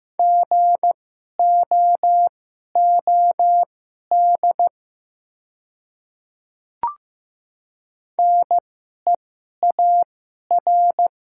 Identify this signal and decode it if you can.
Morse code, heard as a keyed tone, about 15 wpm, spaced as for 11 wpm, 700 Hz. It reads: GOOD NEAR
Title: 2x Character Spacing - Top 200 Words - 15wpm